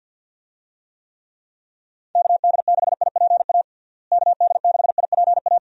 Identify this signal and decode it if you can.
Morse code, heard as a keyed tone, about 50 wpm, 700 Hz. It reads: KD6IPA KD6IPA
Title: USA Call Signs - 6 Characters - Repeated 2x - 50wpm